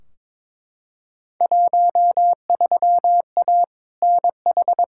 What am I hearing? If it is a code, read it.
Morse code, heard as a keyed tone, about 21 wpm, 700 Hz. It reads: 13A NH
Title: Field Day Exchanges Encore - 22wpm